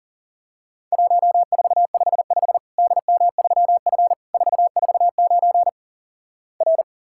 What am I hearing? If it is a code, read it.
Morse code, heard as a keyed tone, about 37 wpm, 700 Hz. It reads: 1455 DM3F 449 R
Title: Worked All Europe DX Contest - QTC Exchange - 40wpm